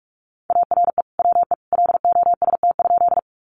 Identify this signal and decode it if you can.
Morse code, heard as a keyed tone, about 44 wpm, 700 Hz. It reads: ARE WE LOST?
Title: Sentences from Top 1400 Words 45wpm